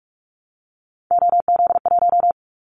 Morse code, heard as a keyed tone, about 46 wpm, 700 Hz. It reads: OZ1
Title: Callsign Prefixes for Top 100 DXCC - 45wpm